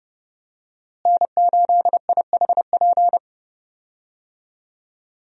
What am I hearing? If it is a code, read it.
Morse code, heard as a keyed tone, about 32 wpm, 700 Hz. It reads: N8IHP
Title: USA Call Signs - 5 Characters - Repeated 3x - 30wpm